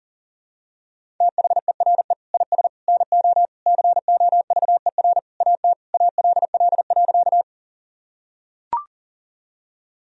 Morse code, heard as a keyed tone, about 37 wpm, 700 Hz. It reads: THERE IS NO COVER AT ALL.
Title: Sentences from Top 1000 Words - Review Part 1 - 40wpm